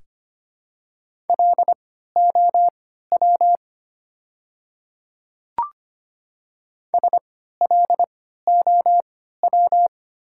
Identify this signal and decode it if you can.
Morse code, heard as a keyed tone, about 25 wpm, spaced as for 14 wpm, 690 Hz. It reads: LOW SLOW
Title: 3x Character Spacing - Top 500 Words - 25wpm